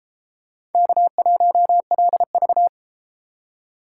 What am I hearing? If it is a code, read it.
Morse code, heard as a keyed tone, about 29 wpm, 700 Hz. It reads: K1LV